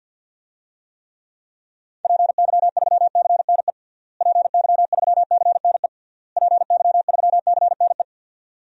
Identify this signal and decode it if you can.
Morse code, heard as a keyed tone, about 52 wpm, 700 Hz. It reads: PY3CNE PY3CNE PY3CNE
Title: DXCC-100 Call Signs - 6 Characters - Repeated 3x - 50wpm